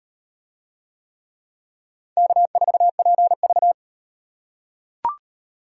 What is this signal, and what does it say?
Morse code, heard as a keyed tone, about 34 wpm, 700 Hz. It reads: K4PV